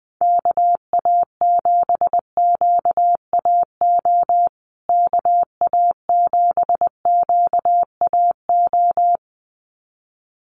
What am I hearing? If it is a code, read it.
Morse code, heard as a keyed tone, about 20 wpm, 700 Hz. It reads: KA7QAO KA7QAO
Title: USA Call Signs Repeated 2x 20wpm